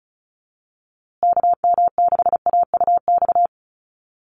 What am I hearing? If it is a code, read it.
Morse code, heard as a keyed tone, about 34 wpm, 700 Hz. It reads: KM6AUX
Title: USA Call Signs Repeated 2x 35wpm